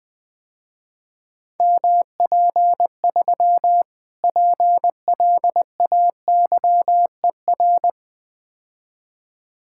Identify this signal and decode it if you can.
Morse code, heard as a keyed tone, about 20 wpm, 700 Hz. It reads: MP3 PLAYER
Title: Difficult Words - Speed-Racing with Hints - 20wpm